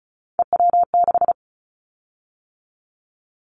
Morse code, heard as a keyed tone, about 36 wpm, 700 Hz. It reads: EW6
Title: Callsign Prefixes for Top 100 DXCC - 35wpm